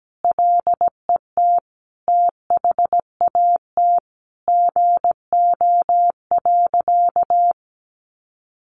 Morse code, heard as a keyed tone, about 17 wpm, 700 Hz. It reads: LET THAT GO.
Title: Sentences from Top 300 Words 17wpm